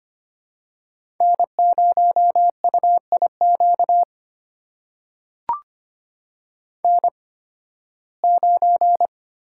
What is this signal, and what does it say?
Morse code, heard as a keyed tone, about 25 wpm, spaced as for 16 wpm, 700 Hz. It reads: N0UIQ N9
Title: USA Call Signs - 5 Characters - 8x Character Spacing - 25wpm